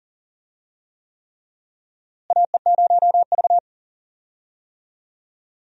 Morse code, heard as a keyed tone, about 40 wpm, 700 Hz. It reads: AE0V
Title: USA Call Signs - 4 Characters - Repeated 3x - 40wpm